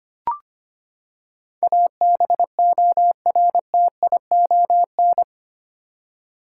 Morse code, heard as a keyed tone, about 25 wpm, 700 Hz. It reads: ABORTION
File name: Eight Letter Words - 25wpm